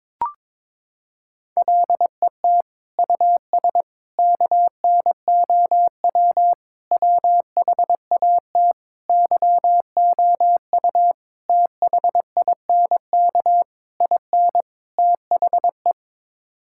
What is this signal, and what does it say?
Morse code, heard as a keyed tone, about 21 wpm, 700 Hz. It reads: LET US KNOW WHAT YOU THINK IN THE
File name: practice / 22wpm / Sets of 8 Words - 22wpm